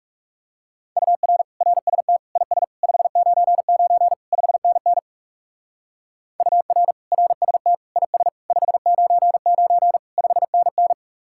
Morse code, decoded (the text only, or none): UR RST IS 599 5NN UR RST IS 599 5NN